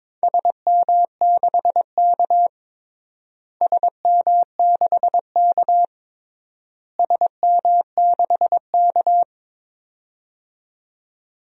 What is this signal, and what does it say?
Morse code, heard as a keyed tone, about 22 wpm, 700 Hz. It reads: SM6K SM6K SM6K
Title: DXCC-100 Call Signs - 4 Characters - Repeated 3x - 22wpm